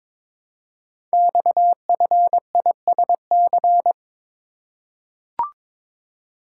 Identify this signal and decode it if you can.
Morse code, heard as a keyed tone, about 23 wpm, 700 Hz. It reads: XFISC
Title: Code Groups - 5 Letters - 22wpm